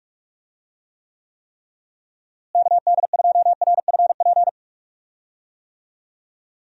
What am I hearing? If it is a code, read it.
Morse code, heard as a keyed tone, about 41 wpm, 700 Hz. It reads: KD2RFP
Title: USA Call Signs - 6 Characters - Repeated 3x - 45wpm